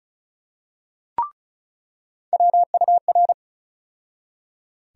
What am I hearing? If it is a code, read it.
Morse code, heard as a keyed tone, about 36 wpm, 700 Hz. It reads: WUR
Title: USA Call Sign Postfixes - 3 Letters 35wpm